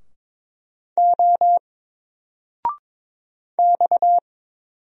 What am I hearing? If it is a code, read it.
Morse code, heard as a keyed tone, about 22 wpm, 700 Hz. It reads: O X
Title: Single Letter-Number 22wpm